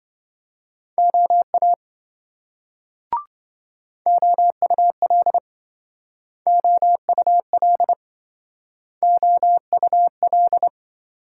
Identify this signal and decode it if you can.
Morse code, heard as a keyed tone, about 26 wpm, 700 Hz. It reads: OA OUL OUL OUL